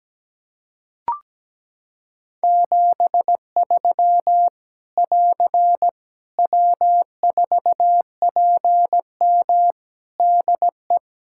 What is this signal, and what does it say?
Morse code, heard as a keyed tone, about 17 wpm, 700 Hz. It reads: 73 <AR> W4PM DE
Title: K1USN SST QSO Practice 17wpm